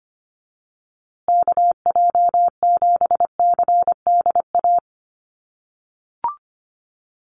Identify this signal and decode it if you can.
Morse code, heard as a keyed tone, about 25 wpm, 700 Hz. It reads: KJ7CDA
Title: USA Call Signs Repeated 2x 25wpm